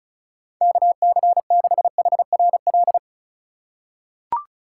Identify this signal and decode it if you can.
Morse code, heard as a keyed tone, about 36 wpm, 700 Hz. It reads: KC6HRL
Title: USA Call Signs - 4x Character Spacing - 35wpm